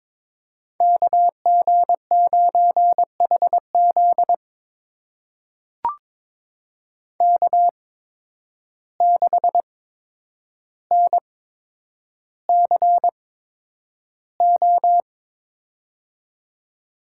Morse code, heard as a keyed tone, about 22 wpm, spaced as for 9 wpm, 700 Hz. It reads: KG9HZ K6NCO
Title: USA Call Signs - 5 Characters - 8x Character Spacing - 22wpm